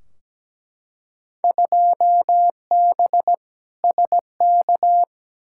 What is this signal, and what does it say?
Morse code, heard as a keyed tone, about 17 wpm, 700 Hz. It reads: 2B SK
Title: Field Day Exchanges Encore - 17wpm